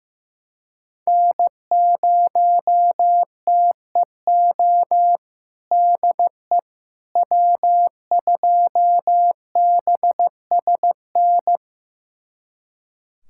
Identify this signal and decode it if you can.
Morse code, heard as a keyed tone, about 16 wpm, 700 Hz. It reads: N0TEO DE W2BSN
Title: USA Callsign DE Callsign - 5 Characters - 15wpm